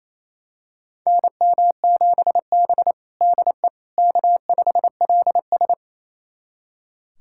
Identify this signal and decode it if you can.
Morse code, heard as a keyed tone, about 27 wpm, 700 Hz. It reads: NM7B DE K5LS
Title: USA Callsign DE Callsign - 4 Characters - 28wpm